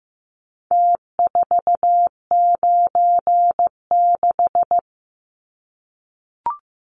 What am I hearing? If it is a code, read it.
Morse code, heard as a keyed tone, about 15 wpm, 700 Hz. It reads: T496
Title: Numbers - Contest Style - 4-Digit 15wpm